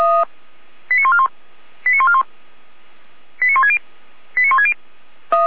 Five-tone selective call (ZVEI), heard as quick runs of five tones, tones of about 70 ms. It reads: 89141 89141 89160 89160
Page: SDR/ZVEI